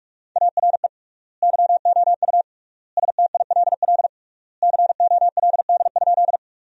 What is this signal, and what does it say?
Morse code, heard as a keyed tone, about 46 wpm, 700 Hz. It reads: ARE YOU STILL COLD?
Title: 3x Word-Spacing - Sentences from Top 400 Words - 45wpm